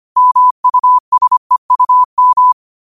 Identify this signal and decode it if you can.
Morse code, heard as a keyed tone, about 25 wpm, 1000 Hz. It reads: MUSEUM
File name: qotdfiles / 25wpm